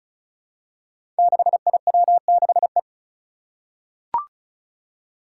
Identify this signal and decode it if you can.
Morse code, heard as a keyed tone, about 36 wpm, 700 Hz. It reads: 6IW6E
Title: Five Challenging Letter-Number Combinations 35wpm